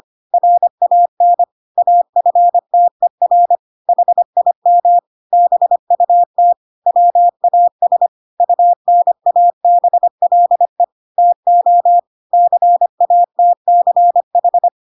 Morse code, heard as a keyed tone, about 25 wpm, 700 Hz. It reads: RAN AFTER HIM BUT WAS UNABLE TO CATCH
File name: Pinocchio / Pinocchio ch3 25 25 ik1hge